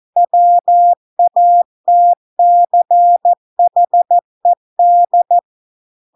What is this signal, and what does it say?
Morse code, heard as a keyed tone, about 14 wpm, 700 Hz. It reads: WATCHED